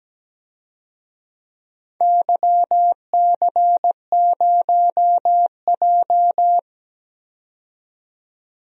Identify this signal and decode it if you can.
Morse code, heard as a keyed tone, about 17 wpm, 700 Hz. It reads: YC0J
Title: DXCC-100 Call Signs - 4 Characters - 17wpm